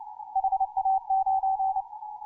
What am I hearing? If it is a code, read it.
Morse code, heard as a keyed tone, about 29 wpm, 770 Hz. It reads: HA9
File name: simple-file-list / hk0na 10mcw